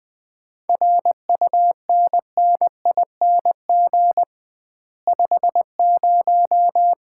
Morse code, heard as a keyed tone, about 21 wpm, 700 Hz. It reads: RUNNING 50
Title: Common QSO Phrases - 2x Word-Spacing - 20wpm